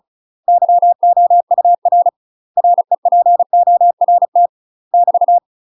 Morse code, heard as a keyed tone, about 34 wpm, 700 Hz. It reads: YOUR REPORT <BT>
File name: UK MASTER QSOs TRAINING/35WPM